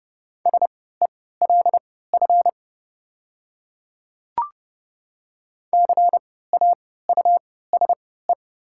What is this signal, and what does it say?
Morse code, heard as a keyed tone, about 32 wpm, spaced as for 17 wpm, 700 Hz. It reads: SELF CAUSE
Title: 3x Character Spacing - Top 300 Words - 30wpm